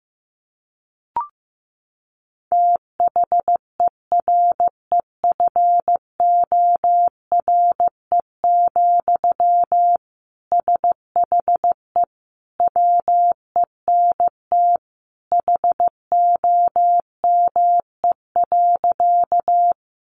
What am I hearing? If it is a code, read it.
Morse code, heard as a keyed tone, about 15 wpm, 700 Hz. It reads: THEREFORE, SHE WENT HOME.
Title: Sentences from Top 800 Words 15wpm